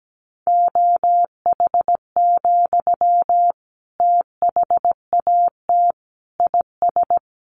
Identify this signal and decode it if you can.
Morse code, heard as a keyed tone, about 17 wpm, 700 Hz. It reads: OH, THAT IS
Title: Sentences from Top 700 Words 17wpm